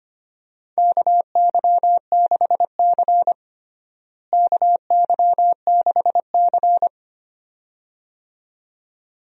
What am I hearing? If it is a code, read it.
Morse code, heard as a keyed tone, about 25 wpm, 700 Hz. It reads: KY6C KY6C
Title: USA Call Signs - 4 Characters - Repeated 2x - 25wpm